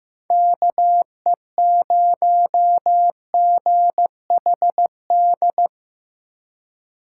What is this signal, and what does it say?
Morse code, heard as a keyed tone, about 18 wpm, 700 Hz. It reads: KE0GHD
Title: Speed Racing - USA 2x3 Call Signs 15wpm